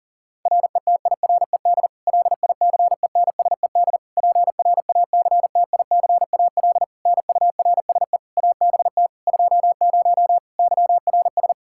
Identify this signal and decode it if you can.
Morse code, heard as a keyed tone, about 40 wpm, 700 Hz. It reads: RETIRED LICENSED PRACTICAL NURSE ABT 20 YRS
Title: Practice QSO - Exchange 3 - More Info - 40wpm